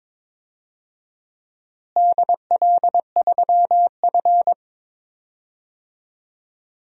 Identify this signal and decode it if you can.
Morse code, heard as a keyed tone, about 22 wpm, 700 Hz. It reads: DL3F
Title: DXCC-100 Call Signs - 4 Characters - Repeated 2x - 22wpm